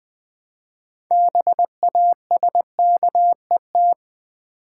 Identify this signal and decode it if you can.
Morse code, heard as a keyed tone, about 20 wpm, 700 Hz. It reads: BASKET
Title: Build A Word - 3 Letters At A Time - 6-Letter Words - 20wpm